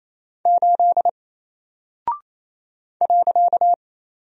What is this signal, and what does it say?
Morse code, heard as a keyed tone, about 27 wpm, 700 Hz. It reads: 8 .
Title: Single Character (Letter:Number:Punctuation:Prosign) 28wpm